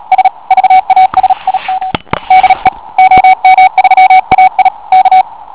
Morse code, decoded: S VASIM DE OM3AI K